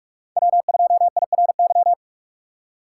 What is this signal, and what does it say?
Morse code, heard as a keyed tone, about 57 wpm, 700 Hz. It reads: W2IRY